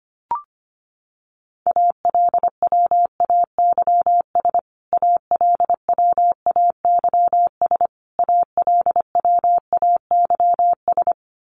Morse code, heard as a keyed tone, about 25 wpm, 700 Hz. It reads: ALWAYS ALWAYS ALWAYS
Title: Top 500 Words Repeated 25wpm